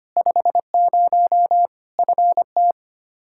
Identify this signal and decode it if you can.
Morse code, heard as a keyed tone, about 25 wpm, 700 Hz. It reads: 50 FT
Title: Practice QSO - Exchange 2 - Info - 25wpm